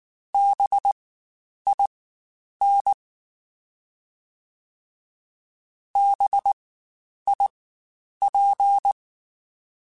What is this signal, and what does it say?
Morse code, heard as a keyed tone, about 19 wpm, spaced as for 9 wpm, 800 Hz. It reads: BIN BIP
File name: Spaced Alphas / Spaced Alphas mp3s / Bs sp @18